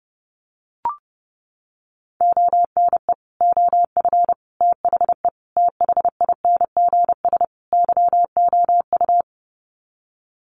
Morse code, heard as a keyed tone, about 30 wpm, 700 Hz. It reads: ONE OF THE THINGS YOU
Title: Sets of 5 Words Encore2 30wpm